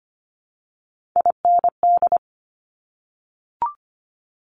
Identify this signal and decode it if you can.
Morse code, heard as a keyed tone, about 25 wpm, 700 Hz. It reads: IND
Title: Top Three Letter Combinations in Words 25wpm